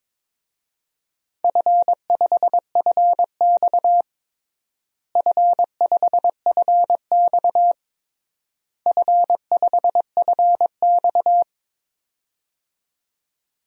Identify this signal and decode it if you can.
Morse code, heard as a keyed tone, about 22 wpm, 700 Hz. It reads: F5FX F5FX F5FX
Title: DXCC-100 Call Signs - 4 Characters - Repeated 3x - 22wpm